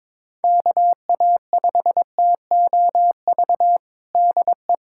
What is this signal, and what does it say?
Morse code, heard as a keyed tone, about 21 wpm, 700 Hz. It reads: KA5TOV DE
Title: USA Callsign DE Callsign - 6 Characters - 22wpm